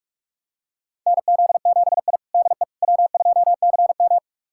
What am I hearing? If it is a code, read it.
Morse code, heard as a keyed tone, about 42 wpm, 690 Hz. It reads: NZ7I DE W2CM
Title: USA Callsign DE Callsign - 4 Characters - 45wpm